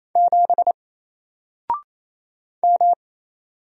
Morse code, heard as a keyed tone, about 28 wpm, 700 Hz. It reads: 7 M
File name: Single-Letter-Number-28wpm